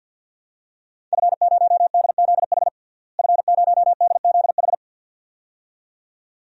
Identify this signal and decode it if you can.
Morse code, heard as a keyed tone, about 47 wpm, 700 Hz. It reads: F0DZH F0DZH
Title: DXCC-100 Call Signs - 5 Characters - Repeated 2x - 50wpm